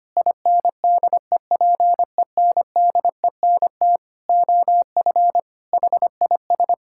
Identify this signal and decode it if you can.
Morse code, heard as a keyed tone, about 25 wpm, 700 Hz. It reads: INDEPENDENT OF HIS
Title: Sentences from Top 1000 Words - Review Part 6 - 25wpm